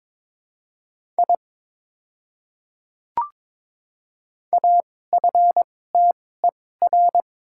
Morse code, heard as a keyed tone, about 22 wpm, spaced as for 16 wpm, 700 Hz. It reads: I AFTER
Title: 2x Character Spacing - Top 300 Words - 22wpm